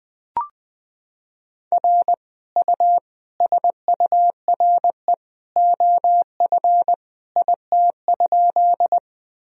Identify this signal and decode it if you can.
Morse code, heard as a keyed tone, about 21 wpm, 700 Hz. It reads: R U SURE OF IT?
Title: Sentences from Top 300 Words and Abbreviations 20wpm